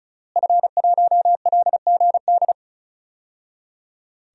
Morse code, heard as a keyed tone, about 34 wpm, 700 Hz. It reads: F1LGD
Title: DXCC-100 Call Signs - 5 Characters - Repeated 3x - 35wpm